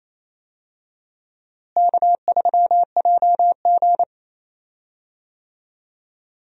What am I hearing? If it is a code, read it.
Morse code, heard as a keyed tone, about 27 wpm, 700 Hz. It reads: K3JG
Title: USA Call Signs - 4 Characters - Repeated 3x - 28wpm